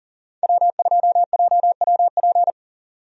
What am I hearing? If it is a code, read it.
Morse code, heard as a keyed tone, about 39 wpm, 700 Hz. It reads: W2JWP
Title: USA Call Signs - 5 Characters - 40wpm